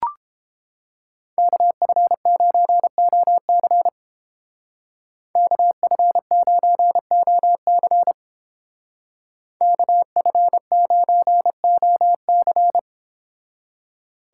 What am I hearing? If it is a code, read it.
Morse code, heard as a keyed tone, about 30 wpm, 700 Hz. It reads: KF9OC KF9OC KF9OC